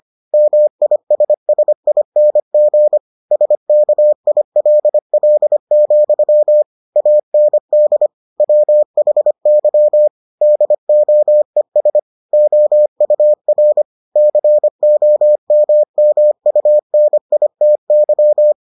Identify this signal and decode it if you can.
Morse code, heard as a keyed tone, about 25 wpm, 600 Hz. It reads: MISSING SKILL, AND WHY DOES OUR COMMUNITY